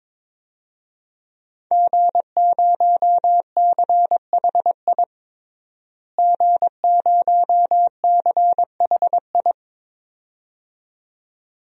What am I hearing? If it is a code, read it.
Morse code, heard as a keyed tone, about 22 wpm, 700 Hz. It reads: G0CHI G0CHI
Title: DXCC-100 Call Signs - 5 Characters - Repeated 2x - 22wpm